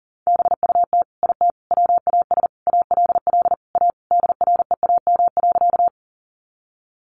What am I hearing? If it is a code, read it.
Morse code, heard as a keyed tone, about 40 wpm, 700 Hz. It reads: BUT IT WAS ALL A DREAM.
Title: Sentences from Top 1500 Words 40wpm